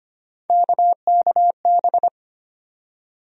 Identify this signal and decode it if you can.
Morse code, heard as a keyed tone, about 25 wpm, 700 Hz. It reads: KKB
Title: USA Call Sign Postfixes - 3 Letters 25wpm